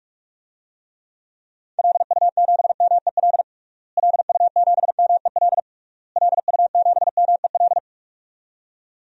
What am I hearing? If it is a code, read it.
Morse code, heard as a keyed tone, about 46 wpm, 700 Hz. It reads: LU7MEL LU7MEL LU7MEL
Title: DXCC-100 Call Signs - 6 Characters - Repeated 3x - 45wpm